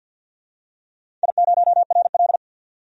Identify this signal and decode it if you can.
Morse code, heard as a keyed tone, about 50 wpm, 700 Hz. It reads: I0RL
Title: DXCC-100 Call Signs - 4 Characters - 50wpm